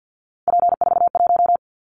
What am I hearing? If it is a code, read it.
Morse code, heard as a keyed tone, about 50 wpm, 700 Hz. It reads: P41
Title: Callsign Prefixes for Top 100 DXCC - 50wpm